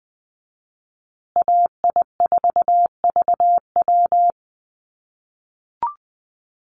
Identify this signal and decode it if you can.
Morse code, heard as a keyed tone, about 20 wpm, 700 Hz. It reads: AI4VW
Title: USA Call Signs Repeated 3x 20wpm